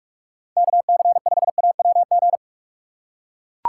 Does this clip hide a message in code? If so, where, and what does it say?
Yes: Morse code from 0.6 to 2.4 s, heard as a keyed tone, about 40 wpm, 700 Hz. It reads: KC5AWG